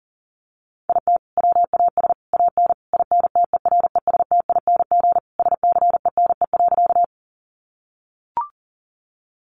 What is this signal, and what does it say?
Morse code, heard as a keyed tone, about 38 wpm, 700 Hz. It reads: IT WAS AN INTERESTING SCENE.
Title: Sentences from Top 1100 Words 40wpm